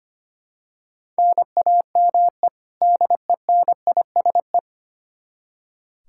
Morse code, heard as a keyed tone, about 25 wpm, 700 Hz. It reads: NAME DENISE
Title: Practice QSO - Exchange 1 - Beginning - 25wpm